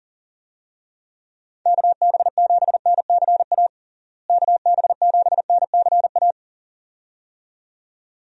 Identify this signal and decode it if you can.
Morse code, heard as a keyed tone, about 40 wpm, 700 Hz. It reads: KB7NCA KB7NCA
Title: USA Call Signs - 6 Characters - Repeated 2x - 40wpm